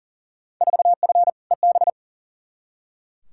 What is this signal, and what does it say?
Morse code, heard as a keyed tone, about 40 wpm, 700 Hz. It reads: 4F EB
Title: Field Day Exchanges - 40wpm